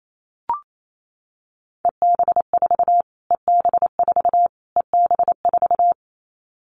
Morse code, heard as a keyed tone, about 28 wpm, 700 Hz. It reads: EB4 EB4 EB4
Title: Callsign Prefixes for Top 25 DXCC - Repeated 3x - 28wpm